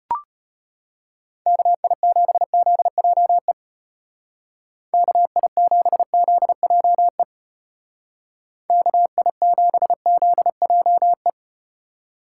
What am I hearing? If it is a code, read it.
Morse code, heard as a keyed tone, about 34 wpm, 700 Hz. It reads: KI7ZJE KI7ZJE KI7ZJE